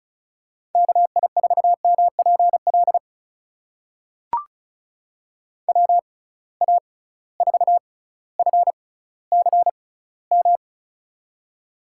Morse code, heard as a keyed tone, about 34 wpm, spaced as for 18 wpm, 700 Hz. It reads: KI4MPL WA4FCM
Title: USA Call Signs - 6 Characters - 6x Character Spacing - 35wpm